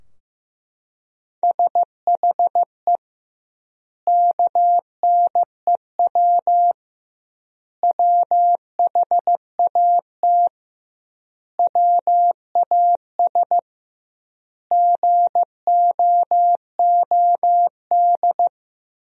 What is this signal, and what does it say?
Morse code, heard as a keyed tone, about 16 wpm, 700 Hz. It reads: SHE KNEW WHAT WAS GOOD
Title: 2x Word-Spacing - Sentences from Top 400 Words - 15wpm